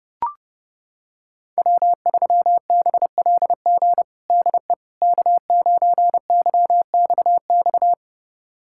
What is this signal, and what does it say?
Morse code, heard as a keyed tone, about 32 wpm, 700 Hz. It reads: W3BLG DE K9YXX
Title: USA Callsign DE Callsign - 5 Characters - 30wpm